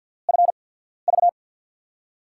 Morse code, heard as a keyed tone, about 53 wpm, spaced as for 14 wpm, 690 Hz. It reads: FV
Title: USA Call Signs - 6 Characters - 8x Character Spacing - 50wpm